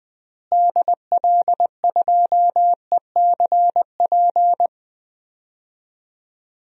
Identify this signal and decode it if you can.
Morse code, heard as a keyed tone, about 20 wpm, 700 Hz. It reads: DL2ECP
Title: DXCC-100 Call Signs - 6 Characters - Repeated 3x - 20wpm